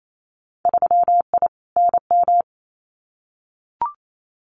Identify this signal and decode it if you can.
Morse code, heard as a keyed tone, about 28 wpm, 700 Hz. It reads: 3I NM
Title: Winter Field Day Exchanges Encore 28wpm